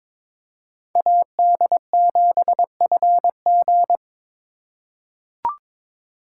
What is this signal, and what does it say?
Morse code, heard as a keyed tone, about 22 wpm, 700 Hz. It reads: AD7FG
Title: USA Call Signs - 5 Characters - 2x Character Spacing - 22wpm